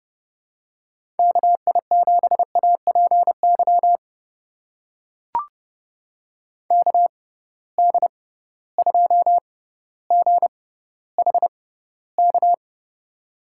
Morse code, heard as a keyed tone, about 31 wpm, spaced as for 15 wpm, 700 Hz. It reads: KI7APY KD2GHK
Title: USA Call Signs - 6 Characters - 6x Character Spacing - 30wpm